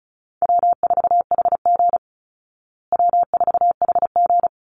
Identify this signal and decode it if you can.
Morse code, heard as a keyed tone, about 34 wpm, 700 Hz. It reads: W4HG W4HG
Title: Call Signs Repeated 3x - 35wpm